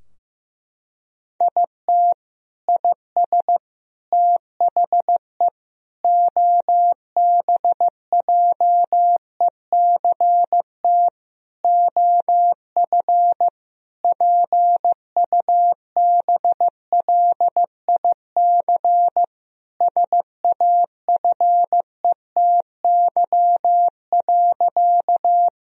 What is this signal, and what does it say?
Morse code, heard as a keyed tone, about 15 wpm, 700 Hz. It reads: IT IS THE OBJECT OF PUBLIC SAFETY.
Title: Sentences from Top 1000 Words - Review Part 4 - 15wpm